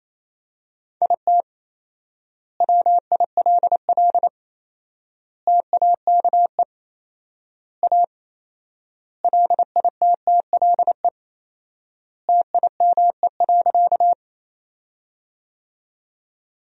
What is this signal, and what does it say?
Morse code, heard as a keyed tone, about 28 wpm, 700 Hz. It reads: IT WILL TAKE A LITTLE TIME.
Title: 4x Word-Spacing - Sentences from Top 200 Words - 28wpm